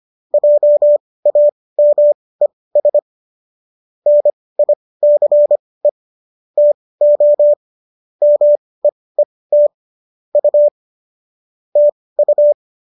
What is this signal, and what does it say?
Morse code, heard as a keyed tone, about 25 wpm, spaced as for 18 wpm, 590 Hz. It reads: JAMES NICE TO MEET U TU